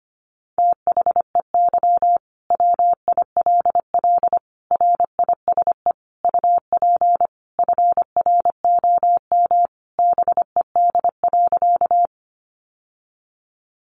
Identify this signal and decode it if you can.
Morse code, heard as a keyed tone, about 25 wpm, 700 Hz. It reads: THEY WILL RISE UP FROM BED.
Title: Sentences from Top 900 Words 25wpm